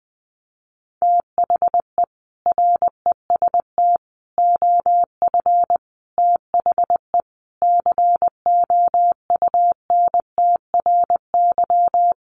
Morse code, heard as a keyed tone, about 20 wpm, 700 Hz. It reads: THE REST OF THE COUNTRY
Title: Sets of 5 Words 20wpm